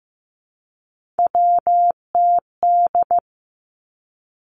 Morse code, heard as a keyed tone, about 15 wpm, 700 Hz. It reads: WTD
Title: Three Challenging Letter-Number Combinations 15wpm